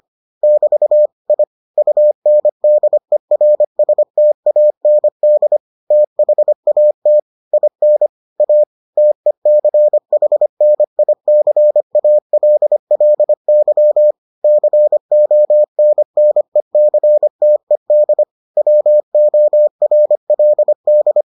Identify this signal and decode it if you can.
Morse code, heard as a keyed tone, about 25 wpm, 600 Hz. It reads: <BT> I UNDERSTAND THAT IN A TECHNICALLY CONNECTED WORLD